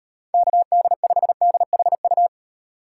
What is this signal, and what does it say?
Morse code, heard as a keyed tone, about 38 wpm, 700 Hz. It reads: KD5DHU